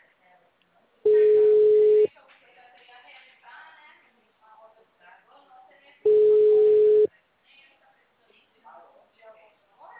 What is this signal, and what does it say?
Ringback tone (Europe, 425 Hz), with speech.
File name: out- -2028- - - .14352